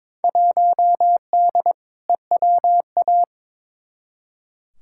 Morse code, heard as a keyed tone, about 22 wpm, 700 Hz. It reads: 1D EWA
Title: Field Day Exchanges - 22wpm